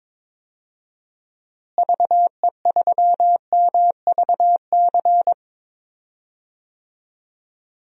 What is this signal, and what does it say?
Morse code, heard as a keyed tone, about 22 wpm, 700 Hz. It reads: VE3MVC
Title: DXCC-100 Call Signs - 6 Characters - Repeated 3x - 22wpm